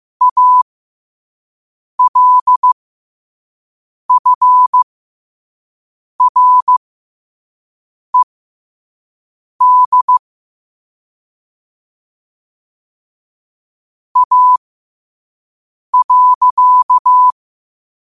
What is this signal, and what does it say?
Morse code, heard as a keyed tone, about 15 wpm, spaced as for 5 wpm, 1000 Hz. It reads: ALFRED A.
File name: qotdfiles / 5wpm